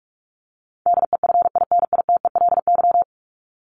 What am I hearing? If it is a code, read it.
Morse code, heard as a keyed tone, about 44 wpm, 700 Hz. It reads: DEFINITELY
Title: Top 1751-2000 Words 45wpm